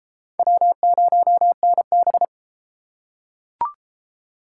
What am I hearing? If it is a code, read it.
Morse code, heard as a keyed tone, about 33 wpm, 700 Hz. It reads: W0NB